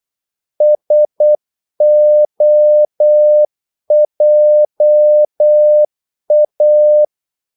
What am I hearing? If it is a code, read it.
Morse code, heard as a keyed tone, about 8 wpm, 600 Hz. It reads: SOJA
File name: Korte ord DK 8wpm